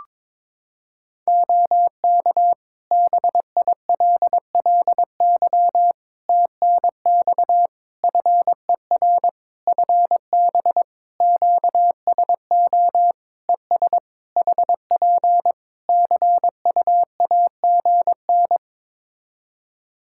Morse code, heard as a keyed tone, about 22 wpm, 700 Hz. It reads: OK BILLY TNX FER FB QSO ES HP CUAGN
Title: Practice QSO - Exchange 4 - Ending - 22wpm